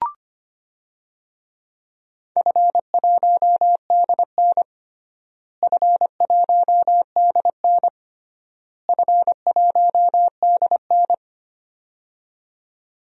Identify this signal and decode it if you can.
Morse code, heard as a keyed tone, about 25 wpm, 700 Hz. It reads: F1DN F1DN F1DN
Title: DXCC-100 Call Signs - 4 Characters - Repeated 3x - 25wpm